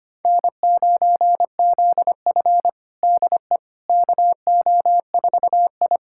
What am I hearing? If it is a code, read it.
Morse code, heard as a keyed tone, about 25 wpm, 700 Hz. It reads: N9ZF DE KO4I
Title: USA Callsign DE Callsign - 4 Characters - 25wpm